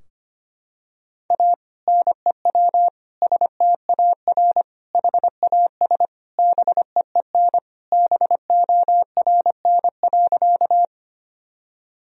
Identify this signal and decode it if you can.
Morse code, heard as a keyed tone, about 25 wpm, 700 Hz. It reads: A NEW STAR HAS BEEN BORN.
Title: Sentences from Top 1000 Words - Review Part 2 - 25wpm